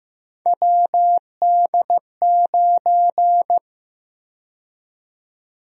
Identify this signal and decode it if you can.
Morse code, heard as a keyed tone, about 15 wpm, 700 Hz. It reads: WD9
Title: USA Call Sign Prefixes - 15wpm